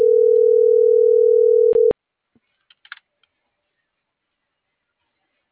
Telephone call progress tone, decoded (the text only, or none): ringback tone